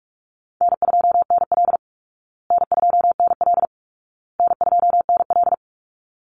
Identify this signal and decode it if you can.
Morse code, heard as a keyed tone, about 44 wpm, 700 Hz. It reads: N2NL N2NL N2NL
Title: Call Signs Repeated 2x - 45wpm